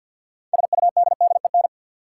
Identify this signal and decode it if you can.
Morse code, heard as a keyed tone, about 53 wpm, 700 Hz. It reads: SUDDEN
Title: Sets of 4 Words - New Word Drill - 50wpm